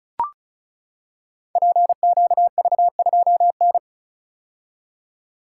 Morse code, heard as a keyed tone, about 35 wpm, 700 Hz. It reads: PQV2N
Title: Code Groups - 5 Letter:Numbers - 35wpm